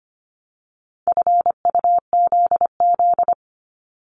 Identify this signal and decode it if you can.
Morse code, heard as a keyed tone, about 25 wpm, 700 Hz. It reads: FUZZ
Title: Multi-letter Phonemes 25wpm